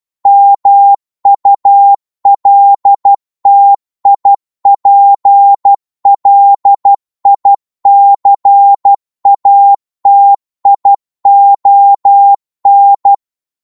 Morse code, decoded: MULTIPLICATION